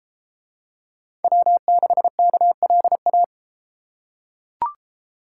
Morse code, heard as a keyed tone, about 32 wpm, 700 Hz. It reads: W6KLA